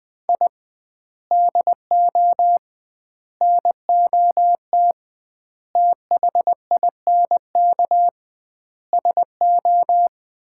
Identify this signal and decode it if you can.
Morse code, heard as a keyed tone, about 20 wpm, 700 Hz. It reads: I DO NOT THINK SO
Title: 2x Word-Spacing - Sets of 5 Words - 20wpm